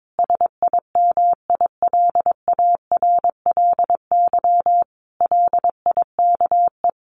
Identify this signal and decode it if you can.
Morse code, heard as a keyed tone, about 22 wpm, 700 Hz. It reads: SIMILARLY LIKE
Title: Sentences from Top 1400 Words 22wpm